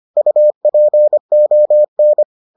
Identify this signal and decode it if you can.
Morse code, heard as a keyed tone, about 25 wpm, 600 Hz. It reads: UPON